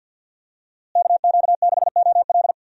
Morse code, heard as a keyed tone, about 51 wpm, 700 Hz. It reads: KQ6OL
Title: USA Call Signs - 5 Characters - Repeated 3x - 50wpm